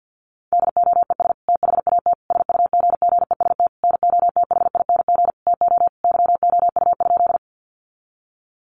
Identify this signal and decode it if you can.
Morse code, heard as a keyed tone, about 50 wpm, 700 Hz. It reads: DOES THAT SUGGEST NOTHING TO YOU?
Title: Sentences from Top 1300 Words 50wpm